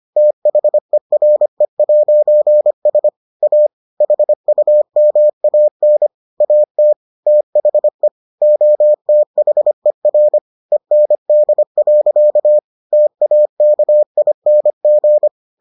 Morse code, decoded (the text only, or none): THERE'S A HUMAN AT THE OTHER END. TAKING